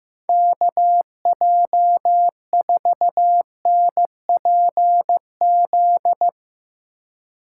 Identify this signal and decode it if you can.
Morse code, heard as a keyed tone, about 15 wpm, 700 Hz. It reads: KJ4NPZ
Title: Speed Racing - USA 2x3 Call Signs 15wpm